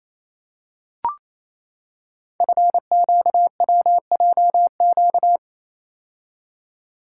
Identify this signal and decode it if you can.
Morse code, heard as a keyed tone, about 27 wpm, 700 Hz. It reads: FQWJQ
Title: Code Groups - 5 Letters - 28wpm